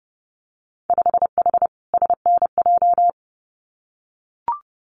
Morse code, heard as a keyed tone, about 30 wpm, 700 Hz. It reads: 5H SNJ
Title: Winter Field Day Exchanges Encore 30wpm